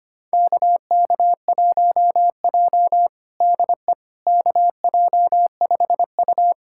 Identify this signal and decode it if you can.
Morse code, heard as a keyed tone, about 25 wpm, 700 Hz. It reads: KK1J DE KJ5U
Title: USA Callsign DE Callsign - 4 Characters - 25wpm